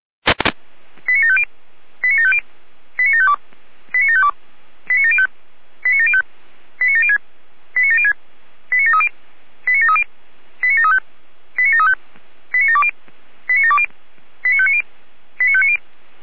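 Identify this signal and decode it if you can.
Five-tone selective call (ZVEI), heard as quick runs of five tones, tones of about 70 ms. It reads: 89750 89750 89752 89752 89705 89705 89706 89706 89730 89730 89735 89735 89720 89720 89590 89590
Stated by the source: ZVEI five-tone selective call received by SDR.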